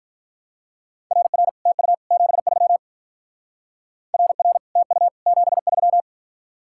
Morse code, heard as a keyed tone, about 45 wpm, 700 Hz. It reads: RR TU 73 RR TU 73